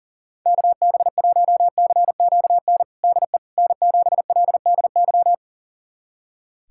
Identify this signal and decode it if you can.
Morse code, heard as a keyed tone, about 42 wpm, 700 Hz. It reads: KB1CQN DE N7LDY
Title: USA Callsign DE Callsign 40wpm